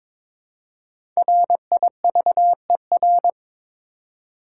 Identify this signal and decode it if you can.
Morse code, heard as a keyed tone, about 22 wpm, 690 Hz. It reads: RIVER
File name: 5 Letter Words - First 3 Letters Given - 22wpm